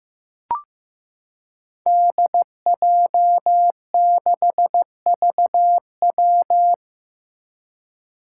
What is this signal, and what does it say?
Morse code, heard as a keyed tone, about 15 wpm, 700 Hz. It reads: DJ6VW
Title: Five Challenging Letter-Number Combinations 15wpm